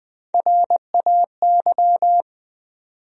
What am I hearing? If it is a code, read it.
Morse code, heard as a keyed tone, about 20 wpm, 700 Hz. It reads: RAY
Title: USA Common Names - Max 4 Letters - 20wpm